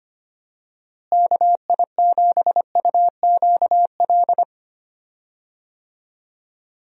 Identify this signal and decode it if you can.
Morse code, heard as a keyed tone, about 25 wpm, 700 Hz. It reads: KI7UQL
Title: USA Call Signs - 6 Characters - Repeated 2x - 25wpm